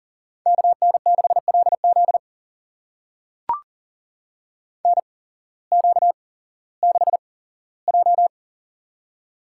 Morse code, heard as a keyed tone, about 40 wpm, spaced as for 19 wpm, 700 Hz. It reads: KN6LZ NQ6J
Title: USA Call Signs - 8x Character Spacing - 40wpm